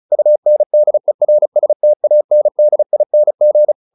Morse code, heard as a keyed tone, about 34 wpm, 600 Hz. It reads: UNDERSTANDING